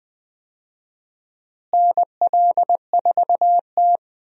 Morse code, heard as a keyed tone, about 20 wpm, 700 Hz. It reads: NL4T
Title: USA Call Signs - 4 Characters - Repeated 3x - 20wpm